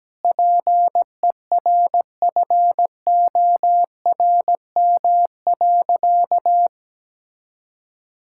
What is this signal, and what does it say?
Morse code, heard as a keyed tone, about 17 wpm, 700 Hz. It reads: PERFORM.
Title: Sentences from Top 1000 Words - Review Part 6 - 17wpm